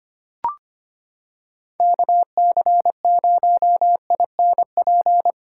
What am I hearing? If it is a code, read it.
Morse code, heard as a keyed tone, about 25 wpm, 700 Hz. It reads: KC0INP
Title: K1USN SST QSO Practice - 25wpm